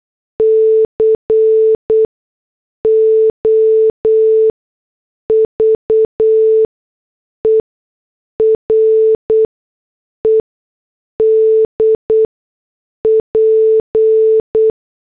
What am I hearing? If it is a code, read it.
Morse code, heard as a keyed tone, about 8 wpm, spaced as for 6 wpm, 440 Hz. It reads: COVERED P
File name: morse / headlines